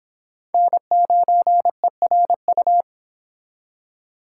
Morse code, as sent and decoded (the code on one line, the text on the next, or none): -. ----. .-. ..-
N9ERU